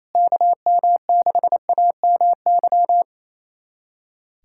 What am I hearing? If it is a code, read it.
Morse code, heard as a keyed tone, about 28 wpm, 700 Hz. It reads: KM6AMY
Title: USA Callsign DE Callsign - 6 Characters - 28wpm